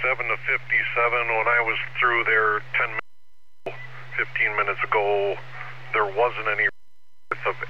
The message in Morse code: none

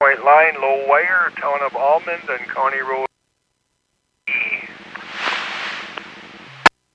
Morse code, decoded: none

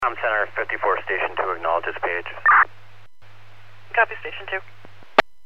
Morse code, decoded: none

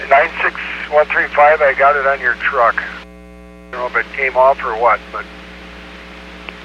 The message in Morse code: none

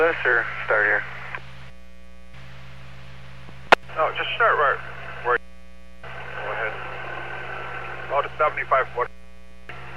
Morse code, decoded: none

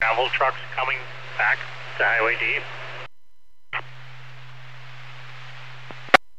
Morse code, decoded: none

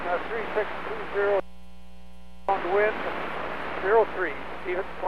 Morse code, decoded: none

